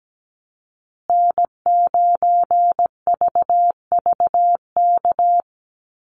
Morse code, heard as a keyed tone, about 17 wpm, 700 Hz. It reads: N9VVK
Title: USA Call Signs Repeated 2x 17wpm